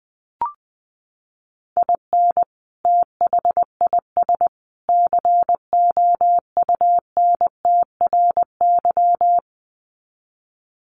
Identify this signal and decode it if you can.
Morse code, heard as a keyed tone, about 20 wpm, 700 Hz. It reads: IN THIS COUNTRY
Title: Sets of 3 Words 20wpm